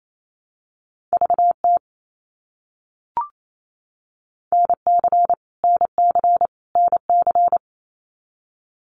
Morse code, heard as a keyed tone, about 28 wpm, 700 Hz. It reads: VT NC NC NC
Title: US States Repeated 3x 28wpm